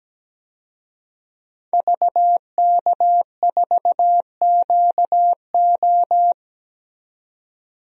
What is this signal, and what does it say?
Morse code, heard as a keyed tone, about 17 wpm, 700 Hz. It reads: VK4QO
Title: DXCC-100 Call Signs - 5 Characters - 17wpm